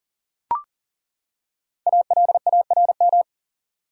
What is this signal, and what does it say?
Morse code, heard as a keyed tone, about 40 wpm, 700 Hz. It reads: ALARM
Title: Five Letter Words - 40wpm